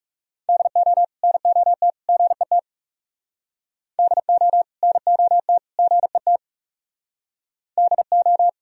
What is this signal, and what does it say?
Morse code, heard as a keyed tone, about 38 wpm, 700 Hz. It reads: DO NOT GET DO NOT GET DO